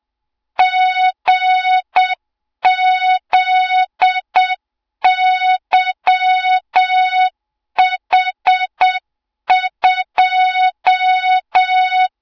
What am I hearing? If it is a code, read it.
Morse code, heard as a keyed tone, about 7 wpm, 3700 Hz, with noise. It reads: GZYH2
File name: Morze-HA2MN / hang / 20 GYAKORLAT